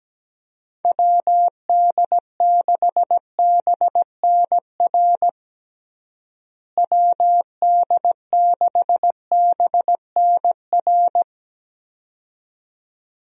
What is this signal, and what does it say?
Morse code, heard as a keyed tone, about 17 wpm, 700 Hz. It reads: WD6BNR WD6BNR
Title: USA Call Signs - 6 Characters - Repeated 2x - 17wpm